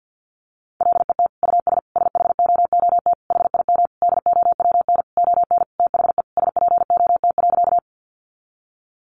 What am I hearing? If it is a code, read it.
Morse code, heard as a keyed tone, about 50 wpm, 700 Hz. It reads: LET US SHOOT HIM DOWN ON THE SPOT.
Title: Sentences from Top 1200 Words 50wpm